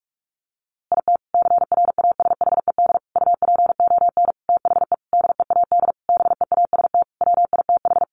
Morse code, heard as a keyed tone, about 44 wpm, 700 Hz. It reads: IT CRASHED UPON THE DEAD BEAST WITH
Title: A Sound Of Thunder 45wpm